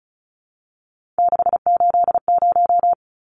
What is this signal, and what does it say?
Morse code, heard as a keyed tone, about 34 wpm, 700 Hz. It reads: 680
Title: Numbers - 3-Digit 35wpm